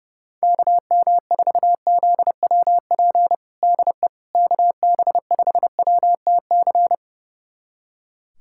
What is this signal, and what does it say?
Morse code, heard as a keyed tone, about 32 wpm, 700 Hz. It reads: KM4ZWP DE KB5WTC
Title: USA Callsign DE Callsign 30wpm